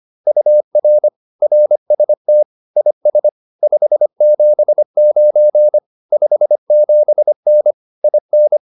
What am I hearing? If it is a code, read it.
Morse code, heard as a keyed tone, about 25 wpm, 600 Hz. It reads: UR RST IS 579 57N IN